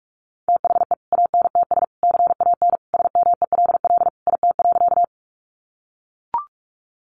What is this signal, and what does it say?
Morse code, heard as a keyed tone, about 44 wpm, 700 Hz. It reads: THE ANTS CAN SMELL IT.
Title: Additional Easy Phrases 45wpm